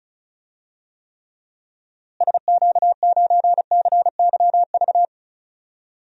Morse code, heard as a keyed tone, about 36 wpm, 700 Hz. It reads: SQ9CYV
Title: DXCC-100 Call Signs - 6 Characters - Repeated 3x - 35wpm